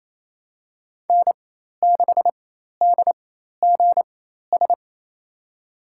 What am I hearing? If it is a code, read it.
Morse code, heard as a keyed tone, about 28 wpm, spaced as for 13 wpm, 700 Hz. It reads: N6DGS
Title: USA Call Signs - 5 Characters - 4x Character Spacing - 28wpm